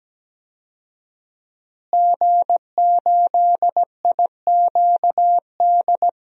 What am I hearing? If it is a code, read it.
Morse code, heard as a keyed tone, about 17 wpm, 700 Hz. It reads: G8IQD
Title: DXCC-100 Call Signs - 5 Characters - Repeated 2x - 17wpm